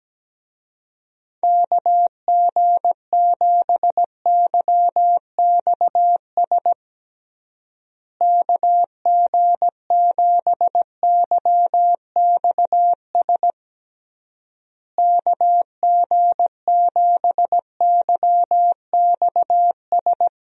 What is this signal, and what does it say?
Morse code, heard as a keyed tone, about 17 wpm, 700 Hz. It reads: KG7YXS KG7YXS KG7YXS
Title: USA Call Signs - 6 Characters - Repeated 3x - 17wpm